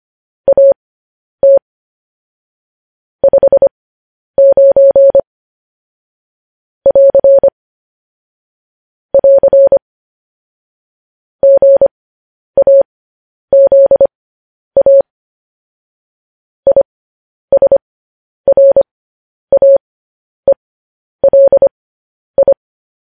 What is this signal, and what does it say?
Morse code, heard as a keyed tone, about 25 wpm, spaced as for 10 wpm, 550 Hz. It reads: AT 59 <AR> <AR> GAZA ISRAELI